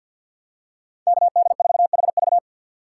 Morse code, heard as a keyed tone, about 52 wpm, 700 Hz. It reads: KD4HV
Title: USA Call Signs - 5 Characters - 2x Character Spacing - 50wpm